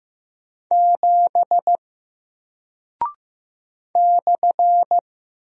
Morse code, heard as a keyed tone, about 15 wpm, 700 Hz. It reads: 7 /
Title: Single Character (Letter:Number:Punctuation:Prosign) 15wpm